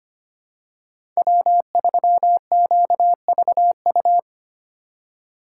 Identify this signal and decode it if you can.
Morse code, heard as a keyed tone, about 25 wpm, 700 Hz. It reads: W3QVU
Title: USA Call Signs - 5 Characters - 3x Character Spacing - 25wpm